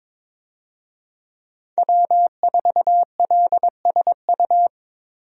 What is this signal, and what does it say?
Morse code, heard as a keyed tone, about 22 wpm, 700 Hz. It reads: W4LSU
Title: USA Call Signs - 5 Characters - Repeated 3x - 22wpm